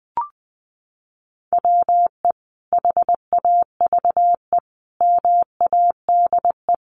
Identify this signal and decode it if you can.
Morse code, heard as a keyed tone, about 20 wpm, 700 Hz. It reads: WE HAVE MADE
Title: Sentences from Top 1400 Words 20wpm